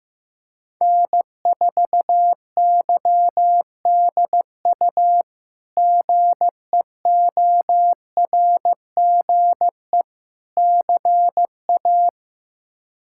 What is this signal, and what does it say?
Morse code, heard as a keyed tone, about 15 wpm, 700 Hz. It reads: N4YDU GEORGE CA
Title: K1USN SST QSO Practice - 15wpm